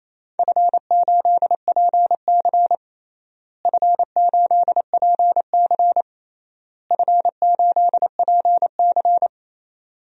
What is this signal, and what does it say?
Morse code, heard as a keyed tone, about 28 wpm, 700 Hz. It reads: F8PC F8PC F8PC
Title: DXCC-100 Call Signs - 4 Characters - Repeated 3x - 28wpm